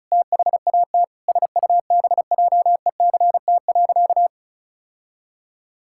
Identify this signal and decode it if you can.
Morse code, heard as a keyed tone, about 36 wpm, 700 Hz. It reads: THAT SUBJECT.
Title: Sentences from Top 1600 Words 35wpm